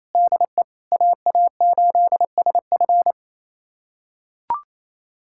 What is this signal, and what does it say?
Morse code, heard as a keyed tone, about 28 wpm, 700 Hz. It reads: DE AA8SF
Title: USA Callsign DE Callsign - 5 Characters - 28wpm